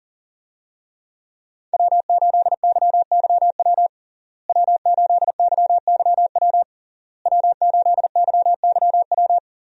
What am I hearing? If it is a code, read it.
Morse code, heard as a keyed tone, about 40 wpm, 700 Hz. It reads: W8YYW W8YYW W8YYW
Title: USA Call Signs - 5 Characters - Repeated 3x - 40wpm